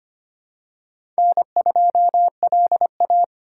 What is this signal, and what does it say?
Morse code, heard as a keyed tone, about 25 wpm, 700 Hz. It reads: N2LA
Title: USA Call Signs - 4 Characters - 2x Character Spacing - 25wpm